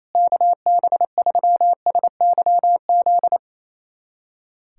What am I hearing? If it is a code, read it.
Morse code, heard as a keyed tone, about 28 wpm, 700 Hz. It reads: KB3SYZ
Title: USA Callsign DE Callsign 28wpm